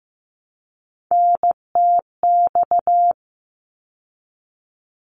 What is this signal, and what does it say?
Morse code, heard as a keyed tone, about 15 wpm, 700 Hz. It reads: NTX
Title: ARRL Sections 15wpm